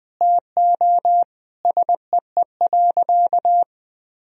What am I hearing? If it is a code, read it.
Morse code, heard as a keyed tone, about 21 wpm, 700 Hz. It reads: TO SEE.
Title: Sentences from Top 500 Words and Abbreviations 20wpm